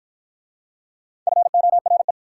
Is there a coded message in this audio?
Yes: Morse code from 1.3 to 2.1 s, heard as a keyed tone, about 50 wpm, 690 Hz. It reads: FORE